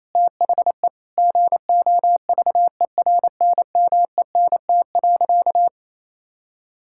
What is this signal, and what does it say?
Morse code, heard as a keyed tone, about 28 wpm, 700 Hz. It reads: THE GOVERNMENT.
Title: Sentences from Top 1600 Words 28wpm